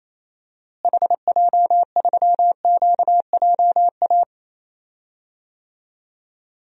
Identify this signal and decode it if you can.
Morse code, heard as a keyed tone, about 28 wpm, 700 Hz. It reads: HJ3QJA
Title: DXCC-100 Call Signs - 6 Characters - Repeated 2x - 28wpm